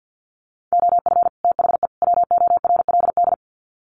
Morse code, heard as a keyed tone, about 50 wpm, 700 Hz. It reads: OF THE WORLD
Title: Sets of 3 Words 50wpm